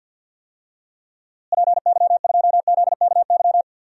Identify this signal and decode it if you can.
Morse code, heard as a keyed tone, about 46 wpm, 700 Hz. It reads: PY2ZKY
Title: DXCC-100 Call Signs - 6 Characters - Repeated 3x - 50wpm